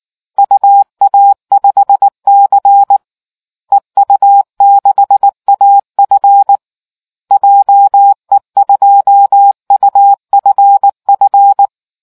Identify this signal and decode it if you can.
Morse code, heard as a keyed tone, about 19 wpm, 800 Hz. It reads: UA5C EU6AF JE2UFF